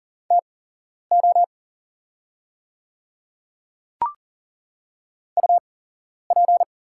Morse code, heard as a keyed tone, about 40 wpm, spaced as for 11 wpm, 690 Hz. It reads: TO UP
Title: 8x Character Spacing - Top 100 Words - 40wpm